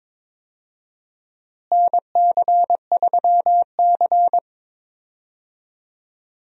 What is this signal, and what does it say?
Morse code, heard as a keyed tone, about 22 wpm, 700 Hz. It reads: NC3C
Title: USA Call Signs - 4 Characters - Repeated 3x - 22wpm